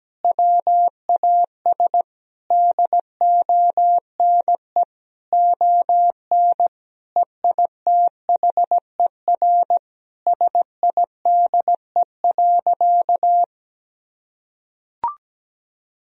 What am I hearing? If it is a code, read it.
Morse code, heard as a keyed tone, about 17 wpm, 700 Hz. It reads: WAS DONE ON EITHER SIDE.
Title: Sentences from Top 1000 Words - Review Part 4 - 17wpm